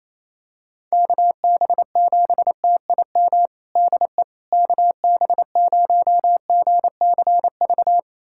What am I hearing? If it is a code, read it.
Morse code, heard as a keyed tone, about 28 wpm, 700 Hz. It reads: KB7TIM DE KB0GCV
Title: USA Callsign DE Callsign - 6 Characters - 28wpm